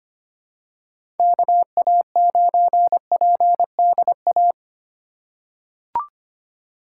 Morse code, heard as a keyed tone, about 25 wpm, 700 Hz. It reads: KA9PDA
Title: USA Call Signs - 6 Characters - 3x Character Spacing - 25wpm